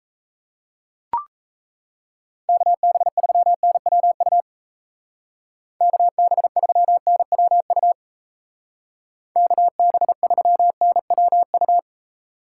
Morse code, heard as a keyed tone, about 35 wpm, 700 Hz. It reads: KB3NWU KB3NWU KB3NWU
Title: Speed Racing - USA 2x3 Call Signs 28wpm